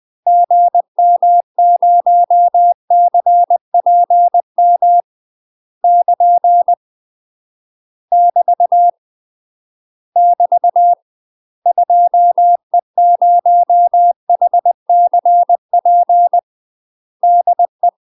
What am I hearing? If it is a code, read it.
Morse code, heard as a keyed tone, about 20 wpm, 700 Hz. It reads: GM0CPM <KN> <BT> <BT> 2E0HCP DE
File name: UK MASTER QSOs TRAINING / 20WPM / UK MASTER QSOs 0051